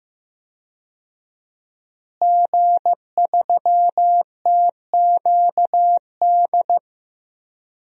Morse code, heard as a keyed tone, about 15 wpm, 700 Hz. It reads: G3TQD
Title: DXCC-100 Call Signs - 5 Characters - Repeated 3x - 15wpm